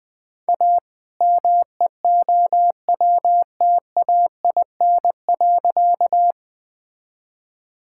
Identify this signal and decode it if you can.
Morse code, heard as a keyed tone, about 20 wpm, 700 Hz. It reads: A MEOWTAIN.
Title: All-Age Appropriate Jokes - 20wpm